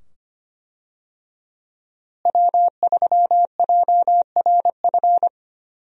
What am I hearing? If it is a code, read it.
Morse code, heard as a keyed tone, about 25 wpm, 700 Hz. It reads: W3JRF
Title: USA Call Signs - 5 Characters - Repeated 2x - 25wpm